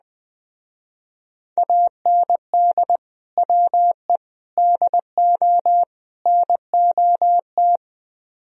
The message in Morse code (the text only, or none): AND WE DO NOT